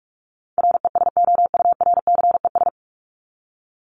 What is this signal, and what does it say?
Morse code, heard as a keyed tone, about 44 wpm, 700 Hz. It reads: RESOURCES
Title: ES:S Suffix - Top 100 Words with Base Words 45wpm